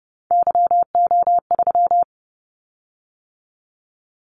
Morse code, heard as a keyed tone, about 30 wpm, 700 Hz. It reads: YO3
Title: Callsign Prefixes for Top 25 DXCC - 30wpm